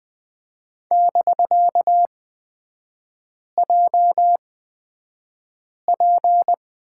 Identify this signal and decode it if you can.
Morse code, heard as a keyed tone, about 20 wpm, spaced as for 5 wpm, 700 Hz. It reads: <BK>JP
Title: 250 Lesson 40e - BK - Three Characters 20wpm